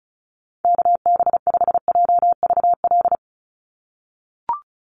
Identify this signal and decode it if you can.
Morse code, heard as a keyed tone, about 34 wpm, 700 Hz. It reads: KB5JVL
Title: USA Call Signs 35wpm